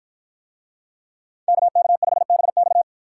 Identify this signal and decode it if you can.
Morse code, heard as a keyed tone, about 51 wpm, 700 Hz. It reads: KK5BX